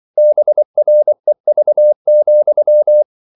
Morse code, heard as a keyed tone, about 24 wpm, 600 Hz. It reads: BREV,
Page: Ferie 24 wpm